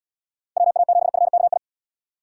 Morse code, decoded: WA7FKI

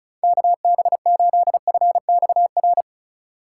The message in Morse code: KB8FXR

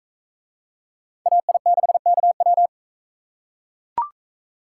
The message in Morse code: AI6KW

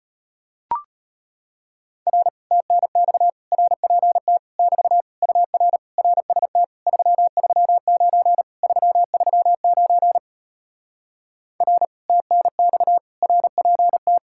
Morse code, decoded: R TNX RPT <BT> UR RST 339 339 R TNX RPT

